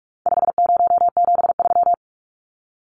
Morse code, heard as a keyed tone, about 44 wpm, 700 Hz. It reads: 5073
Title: Numbers - 4-Digit 45wpm